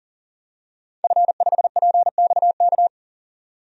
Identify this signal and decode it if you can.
Morse code, heard as a keyed tone, about 40 wpm, 700 Hz. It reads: F5PXK
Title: DXCC-100 Call Signs - 5 Characters - 40wpm